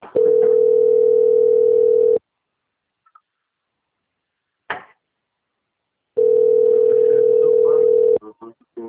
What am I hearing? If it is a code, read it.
Ringback tone (North America), 440 + 480 Hz, with speech.
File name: out- - - - - .16768